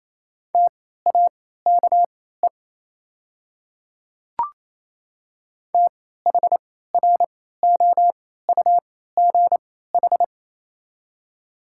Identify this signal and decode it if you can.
Morse code, heard as a keyed tone, about 27 wpm, spaced as for 16 wpm, 690 Hz. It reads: TAKE THROUGH
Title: 3x Character Spacing - Top 200 Words - 28wpm